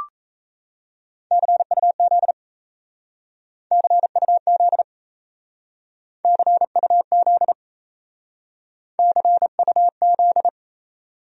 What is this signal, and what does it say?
Morse code, heard as a keyed tone, about 34 wpm, 700 Hz. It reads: CUZ CUZ CUZ CUZ
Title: Speed-Racing Abbreviations - 28wpm